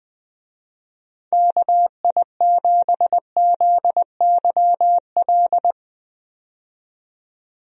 Morse code, heard as a keyed tone, about 20 wpm, 700 Hz. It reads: KI7ZYL
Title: USA Call Signs - 6 Characters - 20wpm